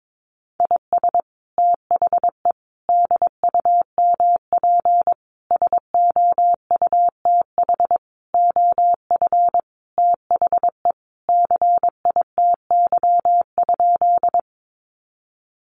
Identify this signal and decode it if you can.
Morse code, heard as a keyed tone, about 22 wpm, 700 Hz. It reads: IS THE DUMP SOUTH OF THE CITY?
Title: Sentences from Top 1300 Words 22wpm